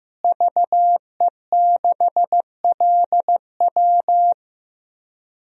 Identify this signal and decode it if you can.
Morse code, heard as a keyed tone, about 15 wpm, 700 Hz. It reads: VE6LW
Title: DXCC-100 Call Signs - 5 Characters - Repeated 3x - 15wpm